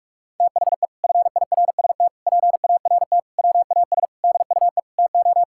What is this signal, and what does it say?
Morse code, heard as a keyed tone, about 46 wpm, 700 Hz. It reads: THE FIRST PART WAS DUE TO
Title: Sentences from Top 1000 Words - Review Part 6 - 45wpm